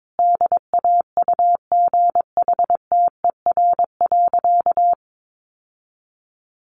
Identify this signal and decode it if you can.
Morse code, heard as a keyed tone, about 22 wpm, 700 Hz. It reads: DAUGHTER.
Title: Sentences from Top 1500 Words 22wpm